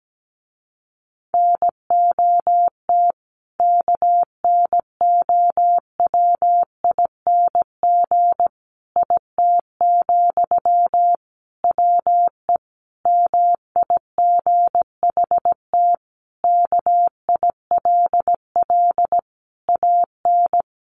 Morse code, heard as a keyed tone, about 17 wpm, 700 Hz. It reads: NOT KNOWING IT, WE MIGHT KILL AN
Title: A Sound Of Thunder 17wpm